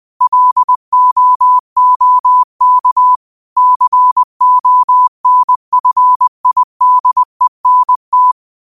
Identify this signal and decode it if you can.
Morse code, heard as a keyed tone, about 20 wpm, 1000 Hz. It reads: LOOK CONFIDENT
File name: qotdfiles / 20wpm